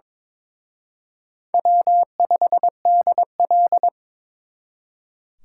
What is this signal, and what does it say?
Morse code, heard as a keyed tone, about 26 wpm, 700 Hz. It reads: W5DL
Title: Speed Racing - USA 1x2 Call Signs 22wpm